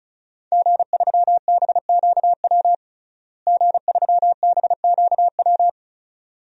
Morse code, heard as a keyed tone, about 34 wpm, 700 Hz. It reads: G3BQW G3BQW
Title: DXCC-100 Call Signs - 5 Characters - Repeated 2x - 35wpm